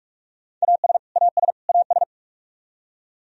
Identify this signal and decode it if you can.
Morse code, heard as a keyed tone, about 46 wpm, 700 Hz. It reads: AS AS AS
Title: Top 100 Words Repeated 45wpm